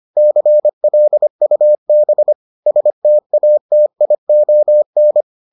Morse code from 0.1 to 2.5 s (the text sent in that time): CLUB